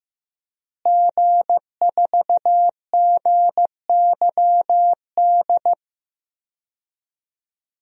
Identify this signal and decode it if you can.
Morse code, heard as a keyed tone, about 16 wpm, 700 Hz. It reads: G4GYD
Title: DXCC-100 Call Signs - 5 Characters - Repeated 2x - 15wpm